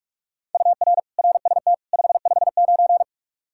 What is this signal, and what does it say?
Morse code, heard as a keyed tone, about 46 wpm, 700 Hz. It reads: UR RST 559
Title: Practice QSO - Exchange 1 - Beginning - 45wpm